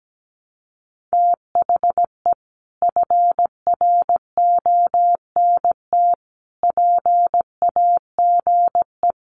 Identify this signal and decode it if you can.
Morse code, heard as a keyed tone, about 17 wpm, 700 Hz. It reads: THE FRONT PAGE
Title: Sets of 5 Words Encore 17wpm